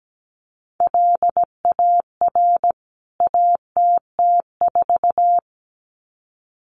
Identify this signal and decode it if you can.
Morse code, heard as a keyed tone, about 17 wpm, 700 Hz. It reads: LAR ATT4
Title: CWT Exchange 17wpm